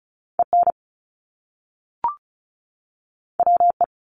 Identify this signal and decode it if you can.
Morse code, heard as a keyed tone, about 36 wpm, 700 Hz. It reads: E N W E
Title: Top Two Letter Combinations in Words 35wpm